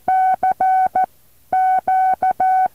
Morse code, heard as a keyed tone, about 14 wpm, spaced as for 10 wpm, 750 Hz. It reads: CQ
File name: intl morse code2